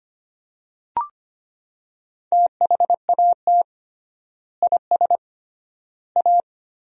Morse code, heard as a keyed tone, about 25 wpm, 700 Hz. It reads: THAT IS A
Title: 3x Word-Spacing - Sentences from Top 100 Words - 25wpm